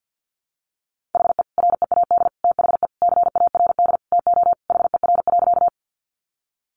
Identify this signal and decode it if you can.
Morse code, heard as a keyed tone, about 50 wpm, 700 Hz. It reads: HE READ THE CARD TO HER.
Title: Sentences from Top 600 Words 50wpm